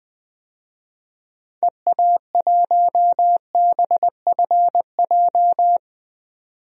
Morse code, heard as a keyed tone, about 20 wpm, 700 Hz. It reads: EA1BFJ
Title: DXCC-100 Call Signs - 6 Characters - 20wpm